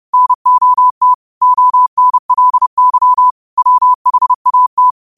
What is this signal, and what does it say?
Morse code, heard as a keyed tone, about 30 wpm, 1000 Hz. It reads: NOT ONLY WHAT